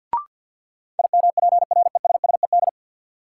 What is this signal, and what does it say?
Morse code, heard as a keyed tone, about 49 wpm, 700 Hz. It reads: IMPRESSED
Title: Sentences from Top 1600 Words - New Word Drill Alt - 50wpm